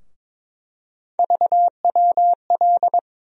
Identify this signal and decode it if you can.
Morse code, heard as a keyed tone, about 22 wpm, 700 Hz. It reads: VWL
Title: USA Call Sign Postfixes - 3 Letters 22wpm